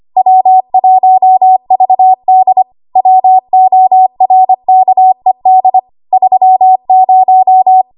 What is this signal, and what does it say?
Morse code, heard as a keyed tone, about 25 wpm, 750 Hz. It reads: W1VD WORKED 30
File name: Morse / 25WPM